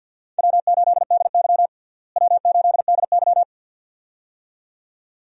Morse code, heard as a keyed tone, about 52 wpm, 700 Hz. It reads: W8DY W8DY
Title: USA Call Signs - 4 Characters - Repeated 2x - 50wpm